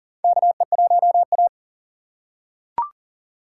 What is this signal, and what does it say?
Morse code, heard as a keyed tone, about 40 wpm, 700 Hz. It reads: KE1A
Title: USA Call Signs - 4 Characters - 6x Character Spacing - 40wpm